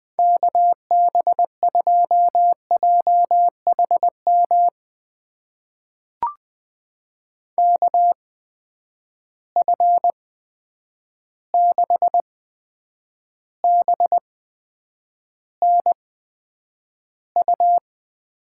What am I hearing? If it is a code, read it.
Morse code, heard as a keyed tone, about 20 wpm, spaced as for 9 wpm, 700 Hz. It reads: KB2JHM KF6BNU
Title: USA Call Signs - 6 Characters - 8x Character Spacing - 20wpm